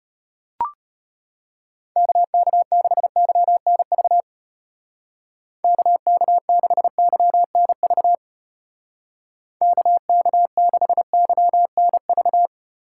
Morse code, heard as a keyed tone, about 32 wpm, 700 Hz. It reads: KK6YNV KK6YNV KK6YNV